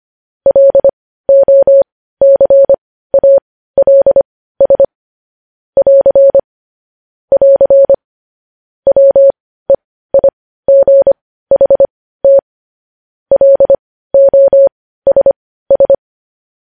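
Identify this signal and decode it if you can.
Morse code, heard as a keyed tone, about 25 wpm, spaced as for 15 wpm, 550 Hz. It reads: LOCALS <AR> <AR> WEIGHT LOSS
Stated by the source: News headlines in Morse code at 15 words per minute, characters sent at 25 WPM.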